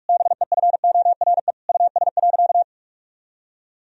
Morse code, heard as a keyed tone, about 46 wpm, 700 Hz. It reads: BEFORE US.
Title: Sentences from Top 1000 Words - Review Part 3 - 45wpm